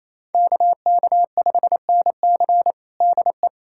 Morse code, heard as a keyed tone, about 28 wpm, 700 Hz. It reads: KK5NC DE
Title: USA Callsign DE Callsign 28wpm